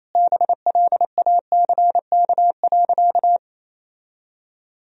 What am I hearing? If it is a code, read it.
Morse code, heard as a keyed tone, about 28 wpm, 700 Hz. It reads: BLACK.
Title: Sentences from Top 1000 Words - Review Part 2 - 28wpm